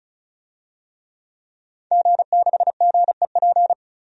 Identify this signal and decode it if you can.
Morse code, heard as a keyed tone, about 34 wpm, 700 Hz. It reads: G6GEP
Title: DXCC-100 Call Signs - 5 Characters - 35wpm